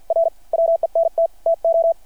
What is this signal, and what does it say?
Morse code, heard as a keyed tone, about 52 wpm, spaced as for 46 wpm, 660 Hz. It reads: R WENT TO
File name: archived / CRSnet